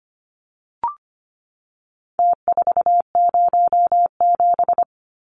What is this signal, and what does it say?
Morse code, heard as a keyed tone, about 25 wpm, 700 Hz. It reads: T407
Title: Numbers - Contest Style - 4-Digit 25wpm